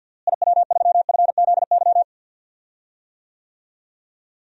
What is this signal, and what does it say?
Morse code, heard as a keyed tone, about 50 wpm, 700 Hz. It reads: IW3FZY
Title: DXCC-100 Call Signs - 6 Characters - Repeated 3x - 50wpm